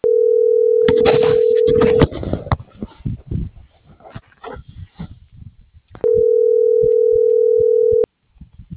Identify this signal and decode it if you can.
Ringback tone (North America), 440 + 480 Hz.